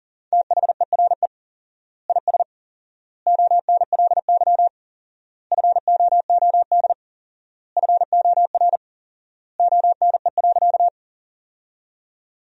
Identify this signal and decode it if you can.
Morse code, heard as a keyed tone, about 40 wpm, 700 Hz. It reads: THERE IS ONLY FOOD FOR ONE.
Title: 4x Word-Spacing - Sentences from Top 300 Words - 40wpm